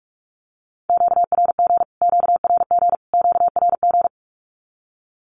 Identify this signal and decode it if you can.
Morse code, heard as a keyed tone, about 47 wpm, 700 Hz. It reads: QRG QRG QRG
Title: Q-Codes Repeated 3x 45wpm